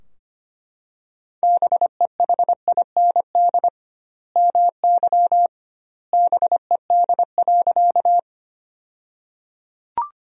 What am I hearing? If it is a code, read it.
Morse code, heard as a keyed tone, about 25 wpm, 700 Hz. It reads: BEHIND MY BED.
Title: 2x Word-Spacing - Sentences from Top 500 Words - 25wpm